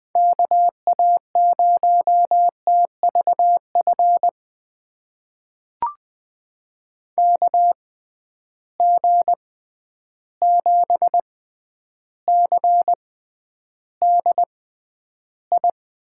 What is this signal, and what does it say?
Morse code, heard as a keyed tone, about 20 wpm, spaced as for 10 wpm, 700 Hz. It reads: KA0TVF KG7CDI
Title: USA Call Signs - 6 Characters - 6x Character Spacing - 20wpm